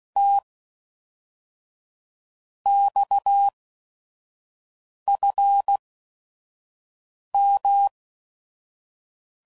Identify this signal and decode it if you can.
Morse code, heard as a keyed tone, about 16 wpm, spaced as for 5 wpm, 800 Hz. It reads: T XFM
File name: CW Code Groups @5WPM